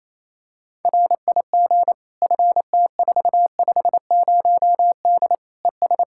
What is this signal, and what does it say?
Morse code, heard as a keyed tone, about 28 wpm, 700 Hz. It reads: RIG FT450D ES
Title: Practice QSO - Exchange 2 - Info - 28wpm